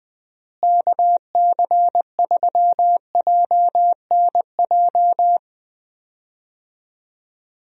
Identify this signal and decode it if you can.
Morse code, heard as a keyed tone, about 21 wpm, 700 Hz. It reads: KC3JNJ
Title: USA Call Signs - 6 Characters - Repeated 2x - 20wpm